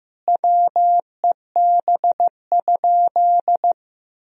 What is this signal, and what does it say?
Morse code, heard as a keyed tone, about 16 wpm, 700 Hz. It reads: WEB?
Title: Sentences from Top 1000 Words - Review Part 4 - 15wpm